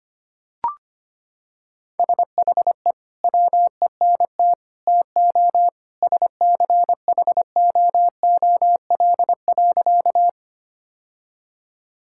Morse code, heard as a keyed tone, about 25 wpm, 700 Hz. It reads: SHE WENT TO SCHOOL.
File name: practice / 25wpm / Sentences with 2-Word Phrases - 25wpm